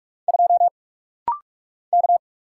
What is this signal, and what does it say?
Morse code, heard as a keyed tone, about 44 wpm, 690 Hz. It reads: 2 K
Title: Single Letter-Number - Rapid-Fire 45wpm